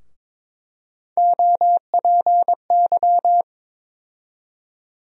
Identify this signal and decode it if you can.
Morse code, heard as a keyed tone, about 22 wpm, 700 Hz. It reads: OPY
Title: USA Call Sign Postfixes - 3 Letters 22wpm